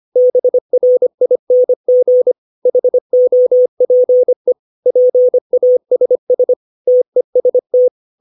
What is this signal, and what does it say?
Morse code, heard as a keyed tone, about 25 wpm, 500 Hz. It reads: BRING HOPE PASS TEST
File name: 632 words / 632 Words 25WPM